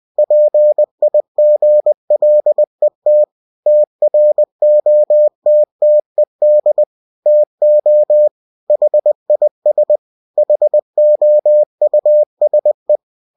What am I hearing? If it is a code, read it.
Morse code, heard as a keyed tone, about 20 wpm, 600 Hz. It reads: PIGLET TROTTED TO HIS HOUSE
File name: cw-books / winnie-the-pooh / winnie-20wpm-page-0038